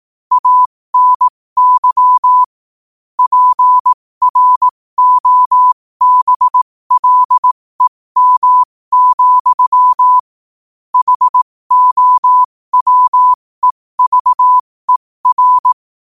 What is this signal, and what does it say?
Morse code, heard as a keyed tone, about 18 wpm, spaced as for 15 wpm, 1000 Hz. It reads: ANY PROBLEM, HOWEVER